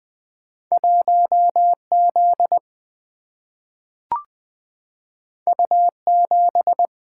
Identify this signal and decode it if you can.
Morse code, heard as a keyed tone, about 20 wpm, 700 Hz. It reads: 1Z U7
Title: Single-Letter Single-Number 20wpm